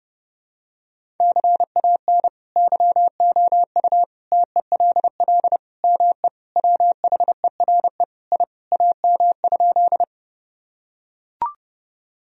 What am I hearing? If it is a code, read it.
Morse code, heard as a keyed tone, about 32 wpm, 700 Hz. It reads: CAN YOU TELL ME WHERE I AM?
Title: Sentences from Top 1000 Words - Review Part 2 - 30wpm